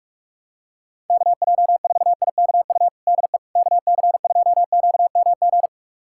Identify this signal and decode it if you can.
Morse code, heard as a keyed tone, about 42 wpm, 700 Hz. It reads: KJ4IKU DE KC2QMG
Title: USA Callsign DE Callsign - 6 Characters - 45wpm